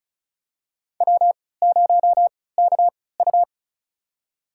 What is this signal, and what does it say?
Morse code, heard as a keyed tone, about 36 wpm, spaced as for 20 wpm, 700 Hz. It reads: W0KU
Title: USA Call Signs - 4 Characters - 3x Character Spacing - 35wpm